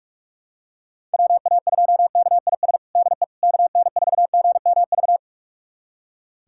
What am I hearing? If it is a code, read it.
Morse code, heard as a keyed tone, about 46 wpm, 690 Hz. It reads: WA2KIS DE KN4GMV
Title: USA Callsign DE Callsign 45wpm